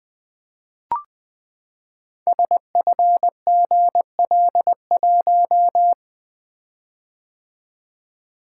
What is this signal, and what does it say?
Morse code, heard as a keyed tone, about 20 wpm, 700 Hz. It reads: SFGL1
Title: Code Groups - 5 Letter:Numbers - 20wpm